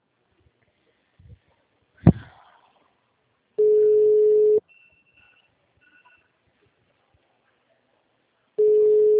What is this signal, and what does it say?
Ringback tone (Europe, 425 Hz).
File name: out- -2001